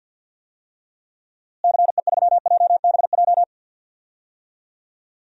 Morse code, heard as a keyed tone, about 47 wpm, 700 Hz. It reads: CE3JBJ
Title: DXCC-100 Call Signs - 6 Characters - Repeated 2x - 50wpm